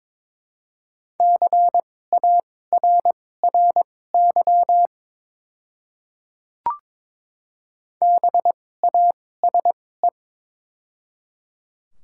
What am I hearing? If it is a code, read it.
Morse code, heard as a keyed tone, about 22 wpm, spaced as for 16 wpm, 700 Hz. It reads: CARRY BASE
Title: 2x Character Spacing - Top 300 Words - 22wpm